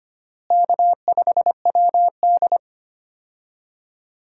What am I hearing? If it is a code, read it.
Morse code, heard as a keyed tone, about 25 wpm, 700 Hz. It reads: K5WD
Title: USA Call Signs - 4 Characters - Repeated 3x - 25wpm